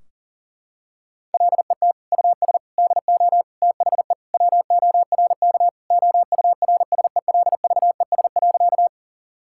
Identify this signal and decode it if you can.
Morse code, heard as a keyed tone, about 37 wpm, 700 Hz. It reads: LET US DO THE WORK OURSELVES.
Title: Sentences from Top 1600 Words 40wpm